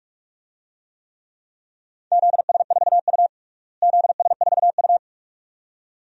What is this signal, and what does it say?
Morse code, heard as a keyed tone, about 46 wpm, 700 Hz. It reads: ZS4U ZS4U
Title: DXCC-100 Call Signs - 4 Characters - Repeated 2x - 45wpm